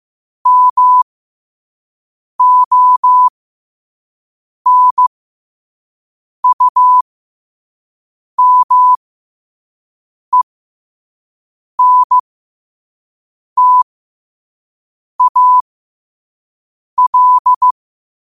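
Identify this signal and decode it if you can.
Morse code, heard as a keyed tone, about 15 wpm, spaced as for 5 wpm, 1000 Hz. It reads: MONUMENTAL
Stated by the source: Morse code practice at 5 wpm.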